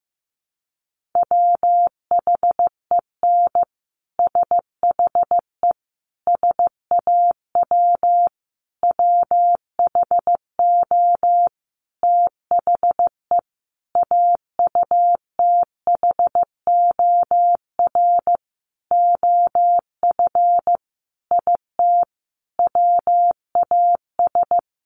Morse code, heard as a keyed tone, about 14 wpm, 700 Hz. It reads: WHEN SHE SAW WHO THE AUTHOR OF IT WAS
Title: Sentences from Top 1200 Words 15wpm